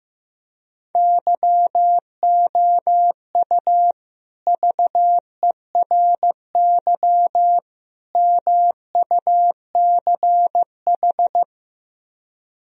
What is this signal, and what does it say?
Morse code, heard as a keyed tone, about 15 wpm, 700 Hz. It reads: YOU VERY MUCH
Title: Speed-Racing - Sets of 3 Words 15wpm